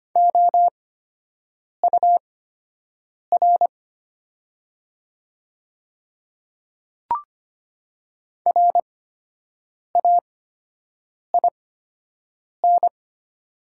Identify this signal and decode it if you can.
Morse code, heard as a keyed tone, about 25 wpm, spaced as for 7 wpm, 700 Hz. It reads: OUR RAIN
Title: 8x Character Spacing - Top 300 Words - 25wpm